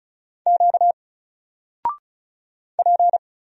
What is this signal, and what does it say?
Morse code, heard as a keyed tone, about 34 wpm, 700 Hz. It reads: Q P
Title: Single Character (Letter:Number:Punctuation:Prosign) 35wpm